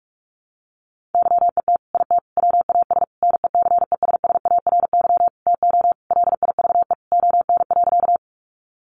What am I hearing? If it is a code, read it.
Morse code, heard as a keyed tone, about 44 wpm, 700 Hz. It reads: YET IT WAS NECESSARY TO LIVE ON.
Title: Sentences from Top 800 Words 45wpm